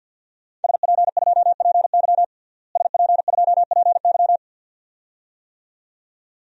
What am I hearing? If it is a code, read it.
Morse code, heard as a keyed tone, about 52 wpm, 700 Hz. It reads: SP2PY SP2PY
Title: DXCC-100 Call Signs - 5 Characters - Repeated 2x - 50wpm